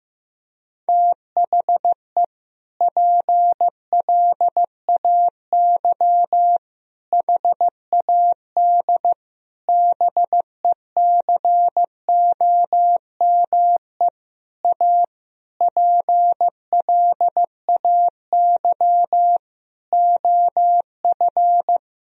Morse code, heard as a keyed tone, about 15 wpm, 700 Hz. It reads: THE PLAY HAD BECOME A PLAY OF
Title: Sentences from Top 1000 Words - Review Part 4 - 15wpm